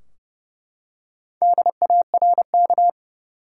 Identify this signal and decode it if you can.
Morse code, heard as a keyed tone, about 31 wpm, 700 Hz. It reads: DARK
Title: Four Letter Words - 30wpm